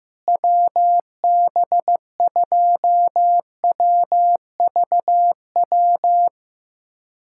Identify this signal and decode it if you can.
Morse code, heard as a keyed tone, about 15 wpm, 700 Hz. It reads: WB2WVW
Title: USA Callsign DE Callsign - 6 Characters - 15wpm